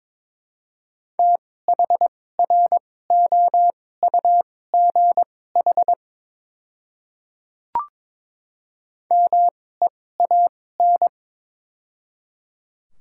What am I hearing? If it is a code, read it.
Morse code, heard as a keyed tone, about 22 wpm, spaced as for 16 wpm, 700 Hz. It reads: THROUGH MEAN
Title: 2x Character Spacing - Top 200 Words - 22wpm